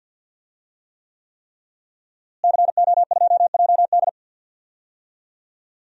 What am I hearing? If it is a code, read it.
Morse code, heard as a keyed tone, about 47 wpm, 700 Hz. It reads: CO2JD
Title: DXCC-100 Call Signs - 5 Characters - Repeated 3x - 50wpm